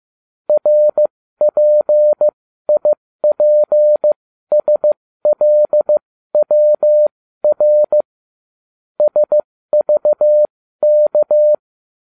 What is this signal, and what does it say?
Morse code, heard as a keyed tone, about 15 wpm, spaced as for 13 wpm, 600 Hz. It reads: RPIPSLWR SVK